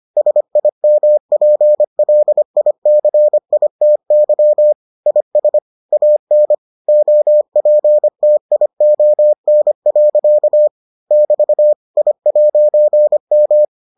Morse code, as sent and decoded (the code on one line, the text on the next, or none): ... .. -- .--. .-.. .. -.-. .. - -.-- / .. ... / .- -. / --- .--. - .. --- -. .-.-.- / -...- / .. .----. --
SIMPLICITY IS AN OPTION. <BT> I'M